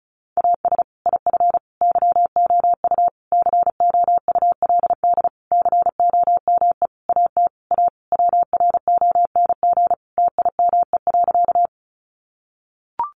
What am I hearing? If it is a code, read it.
Morse code, heard as a keyed tone, about 34 wpm, 700 Hz. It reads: AS IF YOU COULD COME AT A WRONG TIME.
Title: Sentences from Top 900 Words 35wpm